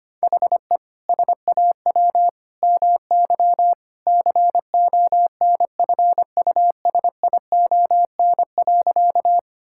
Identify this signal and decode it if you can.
Morse code, heard as a keyed tone, about 25 wpm, 700 Hz. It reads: HE SAW MY CONFUSION.
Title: Sentences from Top 1600 Words 25wpm